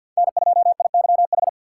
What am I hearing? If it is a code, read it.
Morse code, heard as a keyed tone, about 53 wpm, 700 Hz. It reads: N2IYH
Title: USA Call Signs - 5 Characters - 3x Character Spacing - 50wpm